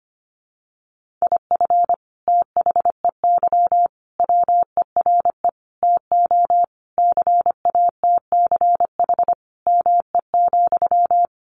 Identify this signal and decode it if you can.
Morse code, heard as a keyed tone, about 25 wpm, 700 Hz. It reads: IF THEY WERE TO CATCH ME,
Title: Sentences from Top 1500 Words 25wpm